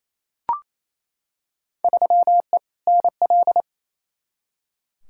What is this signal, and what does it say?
Morse code, heard as a keyed tone, about 28 wpm, 700 Hz. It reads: 3E NL
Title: Field Day Exchanges Encore - 28wpm